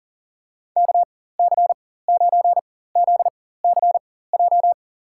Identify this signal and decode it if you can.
Morse code, heard as a keyed tone, about 40 wpm, spaced as for 19 wpm, 700 Hz. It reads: KC9ZCJ
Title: USA Call Signs - 6 Characters - 4x Character Spacing - 40wpm